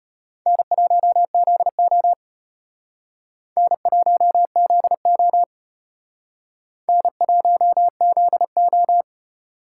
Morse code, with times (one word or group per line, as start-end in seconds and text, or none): 0.5-2.1 N1ZO
3.6-5.4 N1ZO
6.9-9.0 N1ZO